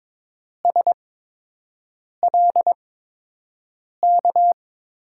Morse code, heard as a keyed tone, about 22 wpm, spaced as for 6 wpm, 700 Hz. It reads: SLK
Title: USA Call Signs - 6 Characters - 8x Character Spacing - 22wpm